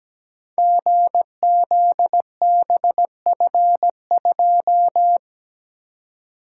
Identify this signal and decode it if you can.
Morse code, heard as a keyed tone, about 17 wpm, 700 Hz. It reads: GZBF2
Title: Code Groups - 5 Letter:Numbers - 17wpm